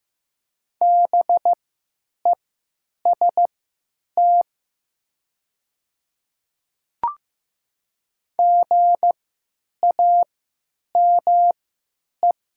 Morse code, heard as a keyed tone, about 15 wpm, spaced as for 8 wpm, 700 Hz. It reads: BEST GAME
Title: 3x Character Spacing - Top 500 Words - 15wpm